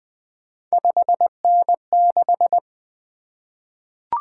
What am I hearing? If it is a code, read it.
Morse code, heard as a keyed tone, about 20 wpm, 700 Hz. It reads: 5N6
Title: Three Letter-Number Combinations - 20wpm